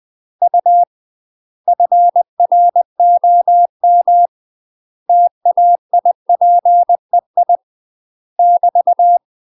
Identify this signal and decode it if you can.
Morse code, heard as a keyed tone, about 20 wpm, 700 Hz. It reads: U FROM TAIPEI <BT>